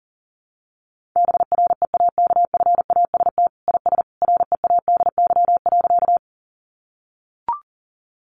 Morse code, heard as a keyed tone, about 40 wpm, 700 Hz. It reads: BREAKFAST IS READY.
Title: Sentences from Top 1500 Words 40wpm